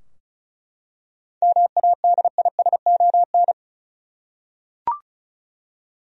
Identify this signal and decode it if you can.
Morse code, heard as a keyed tone, about 36 wpm, 700 Hz. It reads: MADISON
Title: USA Common Names - 35wpm